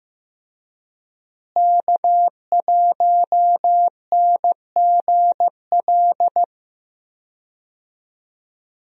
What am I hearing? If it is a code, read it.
Morse code, heard as a keyed tone, about 15 wpm, 700 Hz. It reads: K1NGL
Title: USA Call Signs - 5 Characters - Repeated 3x - 15wpm